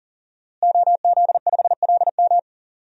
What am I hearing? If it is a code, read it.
Morse code, heard as a keyed tone, about 40 wpm, 700 Hz. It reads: OZ5LM
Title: DXCC-100 Call Signs - 5 Characters - Repeated 3x - 40wpm